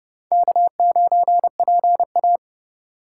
Morse code, heard as a keyed tone, about 28 wpm, 700 Hz. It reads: K9PA